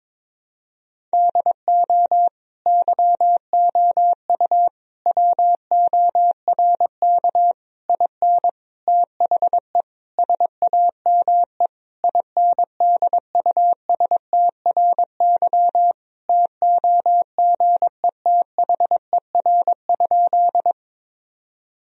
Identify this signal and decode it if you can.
Morse code, heard as a keyed tone, about 21 wpm, 700 Hz. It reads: DO YOU WORK IN THE SAME INDUSTRY TOGETHER?
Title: Sentences from Top 1000 Words - Review Part 4 - 22wpm